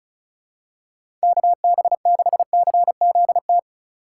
Morse code, heard as a keyed tone, about 36 wpm, 700 Hz. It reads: KB6CZT
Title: USA Call Signs - 6 Characters - 3x Character Spacing - 35wpm